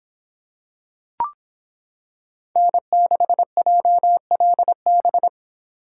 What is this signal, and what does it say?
Morse code, heard as a keyed tone, about 26 wpm, 700 Hz. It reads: N6JLB